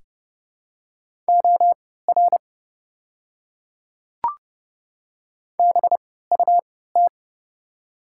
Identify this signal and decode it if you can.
Morse code, heard as a keyed tone, about 32 wpm, spaced as for 17 wpm, 700 Hz. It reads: OR BUT
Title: 3x Character Spacing - Top 100 Words - 30wpm